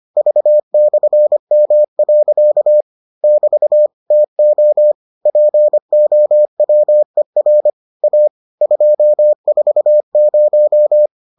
Morse code, heard as a keyed tone, about 25 wpm, 600 Hz. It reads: V/M. <BT> TO POWER A 240